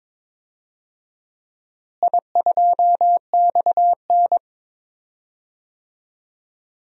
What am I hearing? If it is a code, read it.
Morse code, heard as a keyed tone, about 22 wpm, 700 Hz. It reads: I2XN
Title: DXCC-100 Call Signs - 4 Characters - Repeated 2x - 22wpm